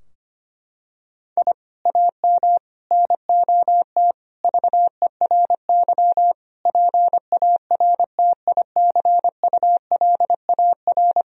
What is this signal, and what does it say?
Morse code, heard as a keyed tone, about 25 wpm, 700 Hz. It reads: I AM NOT VERY PARTICULAR
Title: Sentences from Top 1000 Words - Review Part 4 - 25wpm